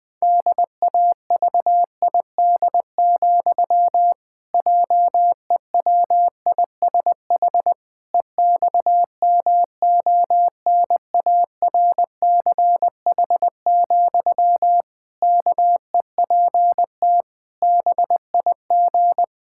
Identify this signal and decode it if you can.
Morse code, heard as a keyed tone, about 20 wpm, 700 Hz. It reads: DAVID, JEWISH EXMONARCH, KEPT BIG
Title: Pangram Challenge v2 - 20wpm